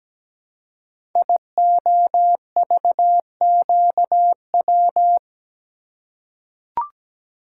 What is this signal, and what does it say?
Morse code, heard as a keyed tone, about 17 wpm, 700 Hz. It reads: IOVQW
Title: Code Groups - 5 Letter:Numbers - 17wpm